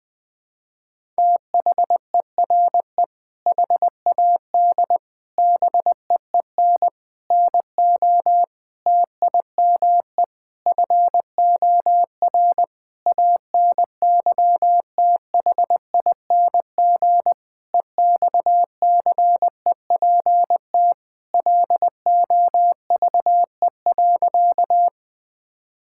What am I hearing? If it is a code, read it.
Morse code, heard as a keyed tone, about 21 wpm, 700 Hz. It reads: THERE HAD BEEN NO TIME FOR ANYTHING EXCEPT LOVE.
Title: Sentences from Top 1600 Words 20wpm